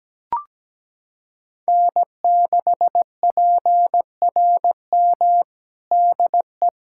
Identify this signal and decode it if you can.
Morse code, heard as a keyed tone, about 17 wpm, 700 Hz. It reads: N6PRM DE
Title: USA Callsign DE Callsign - 5 Characters - 17wpm